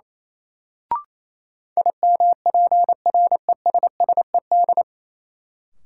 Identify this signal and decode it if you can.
Morse code, heard as a keyed tone, about 28 wpm, 700 Hz. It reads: IMPRESSED
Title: Sentences from Top 1600 Words - New Word Drill - 28wpm